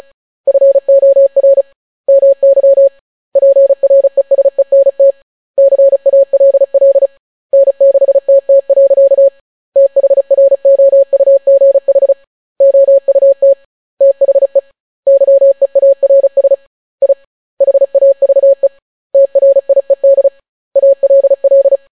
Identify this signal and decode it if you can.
Morse code, heard as a keyed tone, about 36 wpm, 550 Hz. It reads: FOR MY PRESENT CALL N6TT. THROUGH OUT THE YEARS I HAVE TRIED ALL